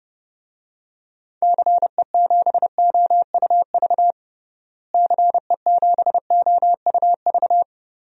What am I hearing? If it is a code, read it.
Morse code, heard as a keyed tone, about 30 wpm, 700 Hz. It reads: CE7OUV CE7OUV
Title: DXCC-100 Call Signs - 6 Characters - Repeated 3x - 30wpm